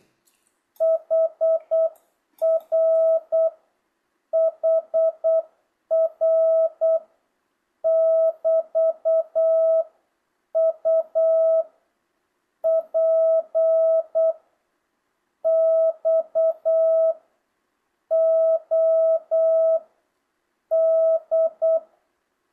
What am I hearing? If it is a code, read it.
Morse code, heard as a keyed tone, about 8 wpm, 660 Hz. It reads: HR HR <BT> U P X O D
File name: jh8cbh / 2025cw